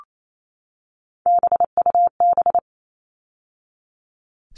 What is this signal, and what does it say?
Morse code, heard as a keyed tone, about 28 wpm, 700 Hz. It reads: BUB
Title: All Cut Numbers - 3-Digit 28wpm